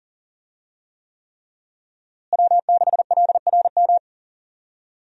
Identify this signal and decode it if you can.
Morse code, heard as a keyed tone, about 40 wpm, 700 Hz. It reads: W6LRM
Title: USA Call Signs - 5 Characters - Repeated 3x - 40wpm